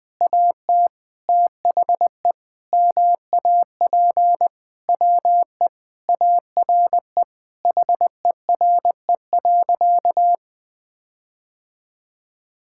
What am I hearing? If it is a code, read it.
Morse code, heard as a keyed tone, about 20 wpm, 700 Hz. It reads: AT THE MAP WE ARE HERE.
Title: Sentences with 5-Word Phrases - 20wpm